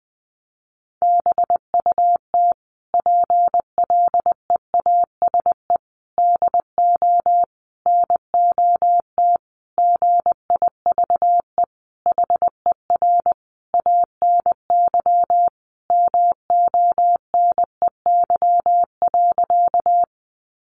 Morse code, heard as a keyed tone, about 20 wpm, 700 Hz. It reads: BUT PLEASE DO NOT GIVE HER ANY MONEY.
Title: Sentences from Top 1500 Words 20wpm